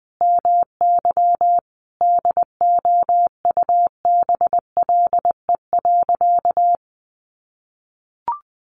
Morse code, heard as a keyed tone, about 20 wpm, 700 Hz. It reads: MY DOUBLE.
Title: Sentences from Top 1400 Words 20wpm